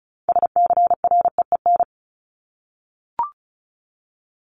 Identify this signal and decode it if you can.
Morse code, heard as a keyed tone, about 34 wpm, 700 Hz. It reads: SCREEN
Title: Top 751-1000 Words 35wpm